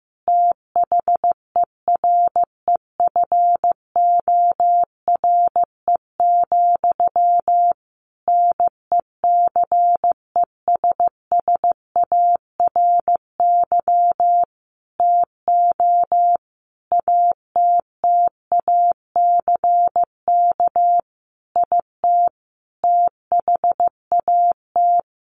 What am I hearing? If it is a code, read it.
Morse code, heard as a keyed tone, about 15 wpm, 700 Hz. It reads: THEREFORE, NECESSARY TO ATTACK IT THAT
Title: Sentences from Top 1500 Words 15wpm